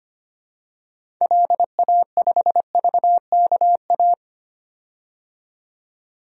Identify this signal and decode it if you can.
Morse code, heard as a keyed tone, about 25 wpm, 700 Hz. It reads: LA5VKA
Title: DXCC-100 Call Signs - 6 Characters - 25wpm